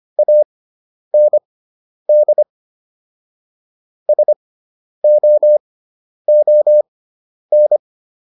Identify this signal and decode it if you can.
Morse code, heard as a keyed tone, about 25 wpm, spaced as for 10 wpm, 600 Hz. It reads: AND SOON